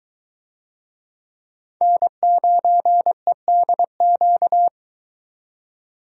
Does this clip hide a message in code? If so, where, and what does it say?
Yes: Morse code from 1.8 to 4.7 s, heard as a keyed tone, about 22 wpm, 700 Hz. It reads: N9EDQ